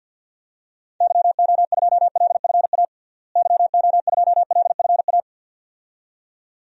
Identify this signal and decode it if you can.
Morse code, heard as a keyed tone, about 53 wpm, 700 Hz. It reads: YO2LFA YO2LFA
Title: DXCC-100 Call Signs - 6 Characters - Repeated 2x - 50wpm